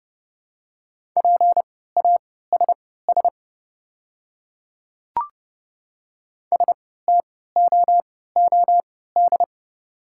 Morse code, heard as a keyed tone, about 30 wpm, spaced as for 17 wpm, 700 Hz. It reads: PASS STOOD
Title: 3x Character Spacing - Top 500 Words - 30wpm